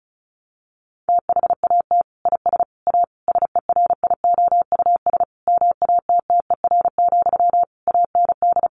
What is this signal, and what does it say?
Morse code, heard as a keyed tone, about 34 wpm, 700 Hz. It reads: THAT IS A SERIOUS MATTER, AND
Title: Sentences from Top 1100 Words 35wpm